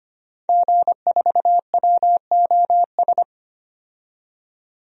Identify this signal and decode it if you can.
Morse code, heard as a keyed tone, about 25 wpm, 700 Hz. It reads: G4WOS
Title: DXCC-100 Call Signs - 5 Characters - 25wpm